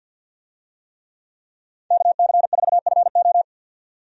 Morse code, heard as a keyed tone, about 52 wpm, 700 Hz. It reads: KC4FO
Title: USA Call Signs - 5 Characters - 50wpm